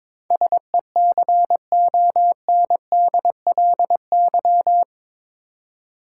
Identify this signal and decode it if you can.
Morse code, heard as a keyed tone, about 22 wpm, 700 Hz. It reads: SECONDLY
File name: practice / 22wpm / Eight Letter Words - 22wpm